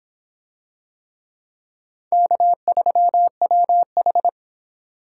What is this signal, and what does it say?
Morse code, heard as a keyed tone, about 25 wpm, 700 Hz. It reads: K3WH